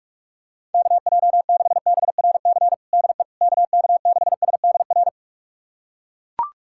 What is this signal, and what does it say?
Morse code, heard as a keyed tone, about 41 wpm, 700 Hz. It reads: KJ6BRC DE KK6SDR
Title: USA Callsign DE Callsign - 6 Characters - 45wpm